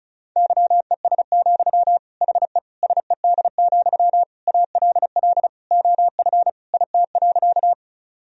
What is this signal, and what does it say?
Morse code, heard as a keyed tone, about 36 wpm, 700 Hz. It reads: YES, HE SED, ALL OF IT.
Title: Sentences from Top 500 Words and Abbreviations 35wpm